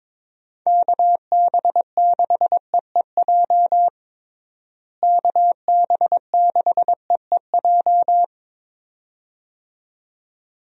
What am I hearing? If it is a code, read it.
Morse code, heard as a keyed tone, about 22 wpm, 700 Hz. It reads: KB6EEJ KB6EEJ
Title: USA Call Signs - 6 Characters - Repeated 2x - 22wpm